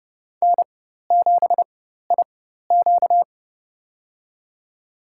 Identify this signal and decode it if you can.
Morse code, heard as a keyed tone, about 30 wpm, spaced as for 14 wpm, 700 Hz. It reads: N7IQ
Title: USA Call Signs - 4 Characters - 4x Character Spacing - 30wpm